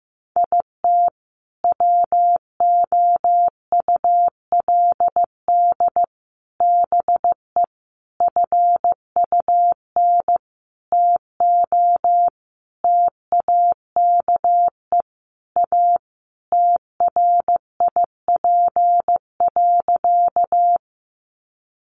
Morse code, heard as a keyed tone, about 15 wpm, 700 Hz. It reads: IT WOULD BE FUN TO TAKE A TRIP.
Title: Sentences from Top 1300 Words 15wpm